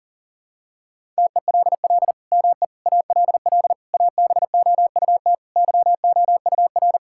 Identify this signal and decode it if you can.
Morse code, heard as a keyed tone, about 37 wpm, 700 Hz. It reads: TELL ME ALL ABOUT YOUR
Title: Sentences from Top 1000 Words - Review Part 5 - 40wpm